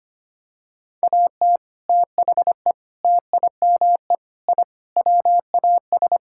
Morse code, heard as a keyed tone, about 25 wpm, 700 Hz. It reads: AT THE TIME I WAS
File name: practice / 25wpm / Sentences with 3-Word Phrases - 25wpm